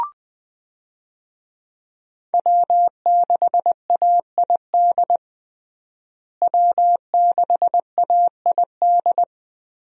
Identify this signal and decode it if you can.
Morse code, heard as a keyed tone, about 20 wpm, 700 Hz. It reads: W6AID W6AID
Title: USA Call Signs - 5 Characters - Repeated 3x - 20wpm